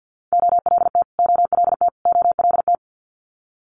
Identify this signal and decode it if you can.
Morse code, heard as a keyed tone, about 50 wpm, 700 Hz. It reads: OLT OLT OLT
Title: Multi-letter Phonemes Repeated 3x 50wpm